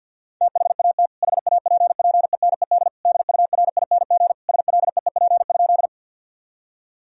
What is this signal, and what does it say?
Morse code, heard as a keyed tone, about 50 wpm, 700 Hz. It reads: THAT HAPPENED DURING SLEEP?
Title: Sentences from Top 1000 Words - Review Part 2 - 50wpm